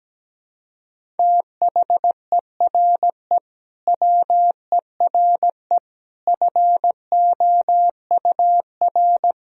Morse code, heard as a keyed tone, about 17 wpm, 700 Hz. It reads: THERE WERE FOUR
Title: Sentences from Top 1000 Words - Review Part 1 - 17wpm